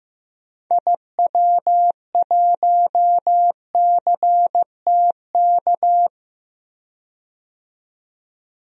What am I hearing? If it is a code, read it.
Morse code, heard as a keyed tone, about 15 wpm, 700 Hz. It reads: IW1CTK
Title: DXCC-100 Call Signs - 6 Characters - 15wpm